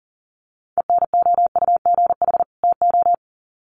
Morse code, heard as a keyed tone, about 40 wpm, 700 Hz. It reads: ENOUGH TO
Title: Sets of 2 Words 40wpm